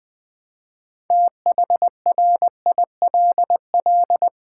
Morse code, heard as a keyed tone, about 20 wpm, 700 Hz. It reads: THRILL
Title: Build A Word - 2 Letters At A Time - 8-Letter Words - 20wpm